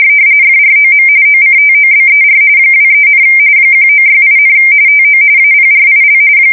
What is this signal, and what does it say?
RTTY, about 45.45 baud. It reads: FROM OCTOBER 23, 1968, TO APRIL
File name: fsk170 / USS-New-Jersey